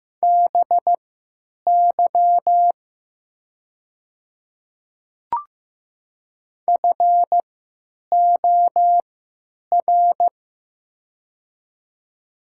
Morse code, heard as a keyed tone, about 15 wpm, spaced as for 8 wpm, 700 Hz. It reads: BY FOR
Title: 3x Character Spacing - Top 100 Words - 15wpm